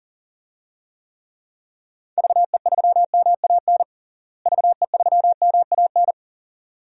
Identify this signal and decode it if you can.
Morse code, heard as a keyed tone, about 40 wpm, 700 Hz. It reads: VE3MAN VE3MAN
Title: DXCC-100 Call Signs - 6 Characters - Repeated 2x - 40wpm